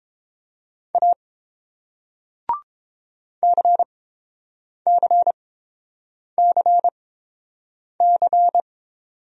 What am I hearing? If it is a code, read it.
Morse code, heard as a keyed tone, about 30 wpm, spaced as for 7 wpm, 700 Hz. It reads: A CCCC